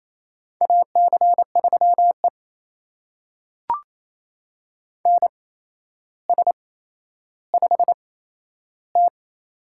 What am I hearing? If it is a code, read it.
Morse code, heard as a keyed tone, about 28 wpm, spaced as for 12 wpm, 700 Hz. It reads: AC3E NS5T
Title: USA Call Signs - 4 Characters - 8x Character Spacing - 28wpm